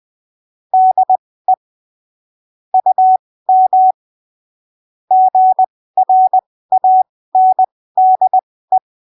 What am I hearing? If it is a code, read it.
Morse code, heard as a keyed tone, about 20 wpm, spaced as for 15 wpm, 750 Hz. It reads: DE UM GRANDE
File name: arquivos / texto / texto 22 20ppm